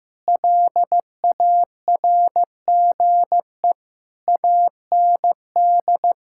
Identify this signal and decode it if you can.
Morse code, heard as a keyed tone, about 16 wpm, 700 Hz. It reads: LARGE AND
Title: Sentences from Top 1600 Words 15wpm